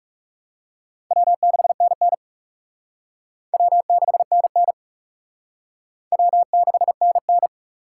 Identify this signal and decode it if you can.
Morse code, heard as a keyed tone, about 36 wpm, 690 Hz. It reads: W6NN W6NN W6NN
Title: Speed Racing - USA Call Signs 30wpm